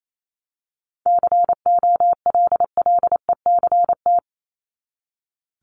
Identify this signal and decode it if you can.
Morse code, heard as a keyed tone, about 28 wpm, 700 Hz. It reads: COLLECT
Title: Top 1501-1750 Words 28wpm